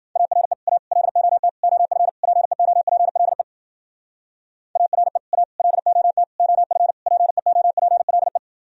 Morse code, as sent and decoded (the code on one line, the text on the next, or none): .- .-. / .- / .-.. --- - / --- ..-. / .--. --- .--. .-.. / .- .-. / .- / .-.. --- - / --- ..-. / .--. --- .--. .-.. .
ARE A LOT OF PEOPLE ARE A LOT OF PEOPLE